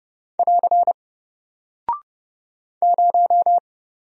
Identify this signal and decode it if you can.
Morse code, heard as a keyed tone, about 30 wpm, 700 Hz. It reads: <AR> 0
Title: Single Character (Letter:Number:Punctuation:Prosign) 30wpm